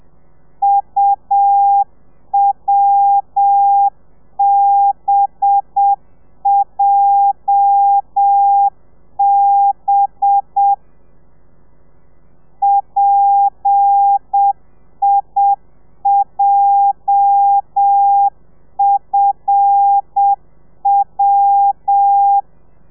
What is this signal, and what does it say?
Morse code, heard as a keyed tone, about 7 wpm, 800 Hz. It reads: UWBJB PIJFW
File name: telegrafia / i2tfj / Lez8 35